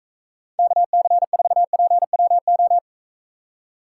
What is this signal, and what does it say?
Morse code, heard as a keyed tone, about 45 wpm, 700 Hz. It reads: KC4PWO